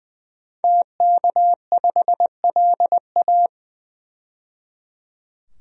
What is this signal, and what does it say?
Morse code, heard as a keyed tone, about 20 wpm, 700 Hz. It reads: TK5LA
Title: DXCC-100 Call Signs - 5 Characters - 20wpm